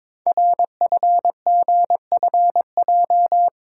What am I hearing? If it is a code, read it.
Morse code, heard as a keyed tone, about 22 wpm, 700 Hz. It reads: RFGFJ
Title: Code Groups - 5 Letters - 22wpm